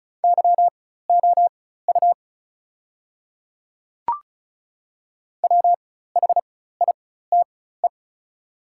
Morse code, heard as a keyed tone, about 36 wpm, spaced as for 16 wpm, 700 Hz. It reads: YOU WHITE
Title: 4x Character Spacing - Top 300 Words - 35wpm